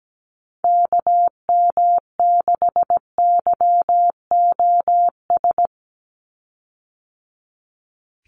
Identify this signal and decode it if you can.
Morse code, heard as a keyed tone, about 17 wpm, 700 Hz. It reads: KM6YOS
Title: USA Call Signs 17wpm